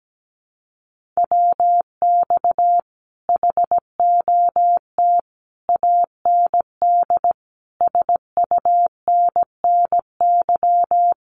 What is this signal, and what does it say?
Morse code, heard as a keyed tone, about 17 wpm, 700 Hz. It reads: WX HOT AND SUNNY
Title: Common QSO Phrases Encore 17wpm